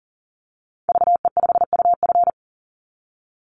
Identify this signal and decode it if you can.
Morse code, heard as a keyed tone, about 41 wpm, 700 Hz. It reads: VE5UF
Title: Call Signs - 40wpm